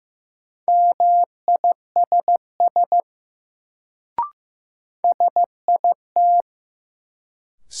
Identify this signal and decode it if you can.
Morse code, heard as a keyed tone, about 15 wpm, 700 Hz. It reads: MISS SIT
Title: Sentences from Top 500 Words - New Word Drill - 15wpm